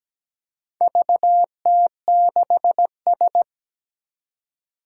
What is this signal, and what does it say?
Morse code, heard as a keyed tone, about 17 wpm, 700 Hz. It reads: VT6S
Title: Four Challenging Letter-Number Combinations 17wpm